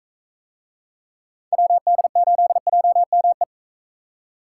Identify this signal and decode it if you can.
Morse code, heard as a keyed tone, about 41 wpm, 700 Hz. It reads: WD8JME